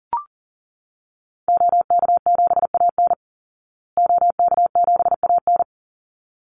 Morse code, heard as a keyed tone, about 40 wpm, 700 Hz. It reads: OK7AN OK7AN
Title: Call Signs Repeated 3x - 40wpm